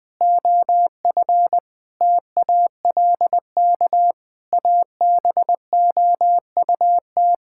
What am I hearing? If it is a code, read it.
Morse code, heard as a keyed tone, about 20 wpm, 700 Hz. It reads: OF TALK ABOUT
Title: Sentences with 7-Word Phrases - 20wpm